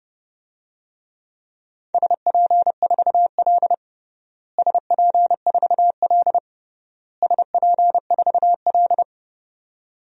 Morse code, heard as a keyed tone, about 32 wpm, 700 Hz. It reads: SP4L SP4L SP4L
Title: DXCC-100 Call Signs - 4 Characters - Repeated 3x - 30wpm